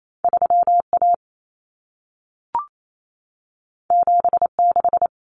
Morse code, heard as a keyed tone, about 28 wpm, 700 Hz. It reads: 3A 76
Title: Common Cut Numbers - 2-Digit 28wpm